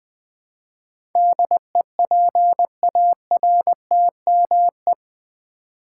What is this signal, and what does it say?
Morse code, heard as a keyed tone, about 20 wpm, 700 Hz. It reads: DEPARTME
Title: Build A Word - 4 Letters At A Time - 10-Letter Words - 20wpm